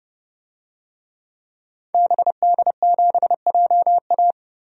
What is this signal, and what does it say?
Morse code, heard as a keyed tone, about 30 wpm, 700 Hz. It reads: BD7JA
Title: DXCC-100 Call Signs - 5 Characters - 30wpm